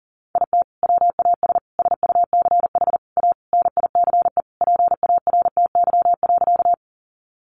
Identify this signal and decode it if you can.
Morse code, heard as a keyed tone, about 37 wpm, 700 Hz. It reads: IT WAS SUCH A NICE PARTY.
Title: Sentences from Top 1400 Words 40wpm